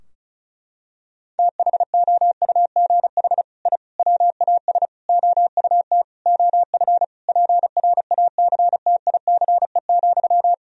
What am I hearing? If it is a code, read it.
Morse code, heard as a keyed tone, about 34 wpm, 700 Hz. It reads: THOUGH I WAS OUT OF PRACTICE,
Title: Sentences from Top 1000 Words - Review Part 4 - 35wpm